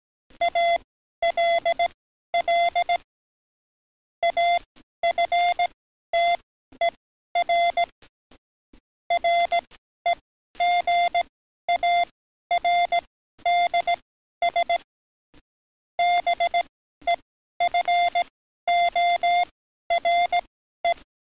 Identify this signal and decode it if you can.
Morse code, heard as a keyed tone, about 18 wpm, spaced as for 12 wpm, 700 Hz. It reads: ALL AFTER REGARDS BEFORE
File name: training / NAVPERS 11049to11060 / NAVPERS 11053 RA RB